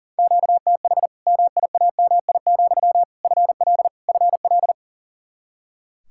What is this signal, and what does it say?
Morse code, heard as a keyed tone, about 40 wpm, 700 Hz. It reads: QTH MIAMI, FL FL
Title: Practice QSO - Exchange 1 - Beginning - 40wpm